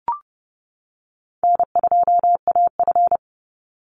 Morse code, heard as a keyed tone, about 30 wpm, 700 Hz. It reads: N2AF
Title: NA SOTA Chaser Callsigns 30wpm